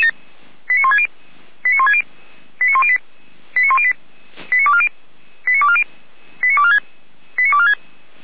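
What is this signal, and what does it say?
Five-tone selective call (ZVEI), heard as quick runs of five tones, tones of about 70 ms. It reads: 89160 89160 89108 89108 89240 89240 89246 89246
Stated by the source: ZVEI five-tone call 89336 received by SDR, archive path 455.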